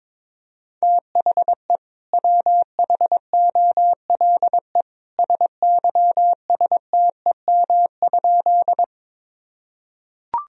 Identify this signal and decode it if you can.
Morse code, heard as a keyed tone, about 22 wpm, 700 Hz. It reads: THE WHOLE SYSTEM?
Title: Sentences from Top 1000 Words - Review Part 3 - 22wpm